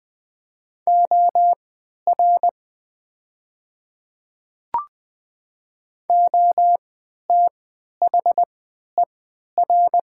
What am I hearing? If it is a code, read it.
Morse code, heard as a keyed tone, about 20 wpm, spaced as for 11 wpm, 700 Hz. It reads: OR OTHER
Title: 3x Character Spacing - Top 100 Words - 20wpm